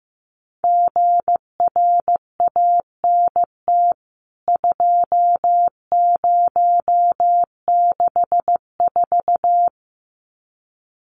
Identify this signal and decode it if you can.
Morse code, heard as a keyed tone, about 15 wpm, 700 Hz. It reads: GRANT 2064
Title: CWT Exchange 15wpm